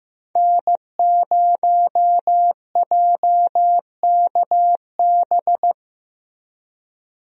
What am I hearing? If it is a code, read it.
Morse code, heard as a keyed tone, about 16 wpm, 700 Hz. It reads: N0JKB
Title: USA Call Signs - 5 Characters - Repeated 2x - 15wpm